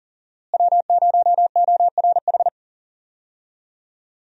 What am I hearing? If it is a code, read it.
Morse code, heard as a keyed tone, about 40 wpm, 700 Hz. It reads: W0ORH
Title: USA Call Signs - 5 Characters - Repeated 3x - 40wpm